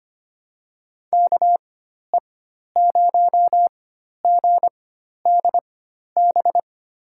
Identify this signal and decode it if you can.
Morse code, heard as a keyed tone, about 25 wpm, spaced as for 12 wpm, 700 Hz. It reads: KE0GDB
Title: USA Call Signs - 4x Character Spacing - 25wpm